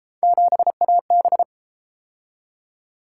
Morse code, heard as a keyed tone, about 33 wpm, 700 Hz. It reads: 7AB